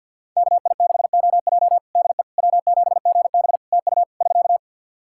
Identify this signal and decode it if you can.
Morse code, heard as a keyed tone, about 51 wpm, 700 Hz. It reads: KI6OJ DE W7GB TU <SK>
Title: Practice QSO - Exchange 4 - Ending - 50wpm